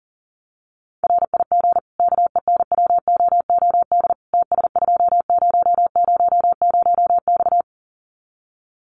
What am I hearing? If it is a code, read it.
Morse code, heard as a keyed tone, about 40 wpm, 700 Hz. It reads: RIG KENWOOD TS2000X
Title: Common QSO Phrases 40wpm